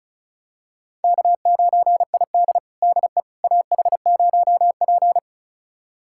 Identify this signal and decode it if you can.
Morse code, heard as a keyed tone, about 36 wpm, 700 Hz. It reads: K9ID DE AH0P
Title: USA Callsign DE Callsign - 4 Characters - 35wpm